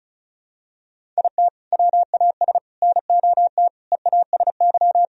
Morse code, heard as a keyed tone, about 34 wpm, 700 Hz. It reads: IT WAS NOT EASY
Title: Sentences from Top 1000 Words - Review Part 2 - 35wpm